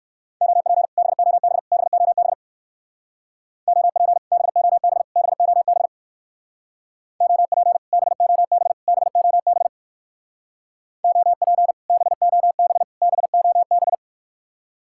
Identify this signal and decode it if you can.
Morse code, heard as a keyed tone, about 57 wpm, 700 Hz. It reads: OP BOB BOB OP BOB BOB OP BOB BOB OP BOB BOB